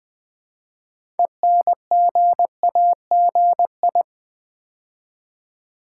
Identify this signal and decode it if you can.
Morse code, heard as a keyed tone, about 20 wpm, 700 Hz. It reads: ENGAGI
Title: Build A Word - 3 Letters At A Time - 8-Letter Words - 20wpm